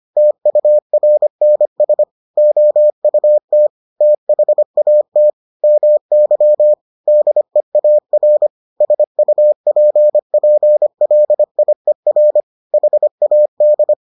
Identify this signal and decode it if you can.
Morse code, heard as a keyed tone, about 25 wpm, 600 Hz. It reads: TURNS OUT THAT MY DEAR SUPPLIER HAD